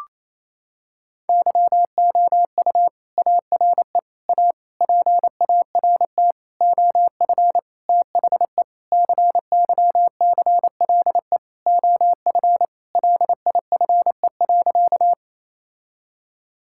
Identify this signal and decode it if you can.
Morse code, heard as a keyed tone, about 27 wpm, 700 Hz. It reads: YOU ARE A PART OF THE CYCLE OF LIFE.
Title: Sentences from Top 1000 Words - Review Part 6 - 28wpm